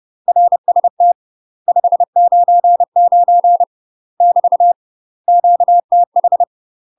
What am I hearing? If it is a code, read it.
Morse code, heard as a keyed tone, about 30 wpm, 700 Hz. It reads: RST 599 <BT> QTH
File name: UK MASTER QSOs TRAINING / 30WPM / UK MASTER QSOs 0045